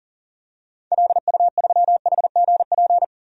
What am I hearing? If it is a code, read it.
Morse code, heard as a keyed tone, about 40 wpm, 700 Hz. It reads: LU3HGP
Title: DXCC-100 Call Signs - 6 Characters - 40wpm